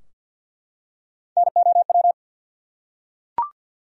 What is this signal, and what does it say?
Morse code, heard as a keyed tone, about 52 wpm, 700 Hz. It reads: NOW
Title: Top 100 Words Repeated 50wpm